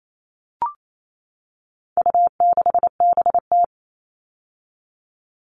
Morse code, heard as a keyed tone, about 28 wpm, 700 Hz. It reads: U6BT
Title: All Cut Numbers - 4-Digit 28wpm